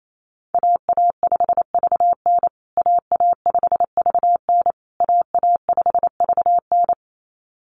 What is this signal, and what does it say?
Morse code, heard as a keyed tone, about 28 wpm, 700 Hz. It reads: AA5VN AA5VN AA5VN
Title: USA Call Signs Repeated 3x 28wpm